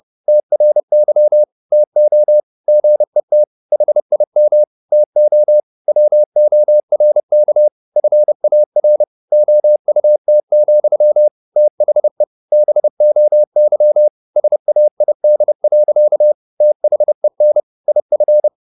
Morse code, heard as a keyed tone, about 30 wpm, 600 Hz. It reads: TRY TO GET HIM TO WORK FAR OUT, THE BOY SAID. THEN IF
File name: the old man and the sea / the old man and the sea 1 30wpm